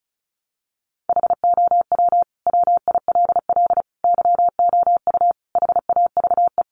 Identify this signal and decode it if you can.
Morse code, heard as a keyed tone, about 34 wpm, 700 Hz. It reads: HOW WILL YOU HAVE
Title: Sentences from Top 900 Words 35wpm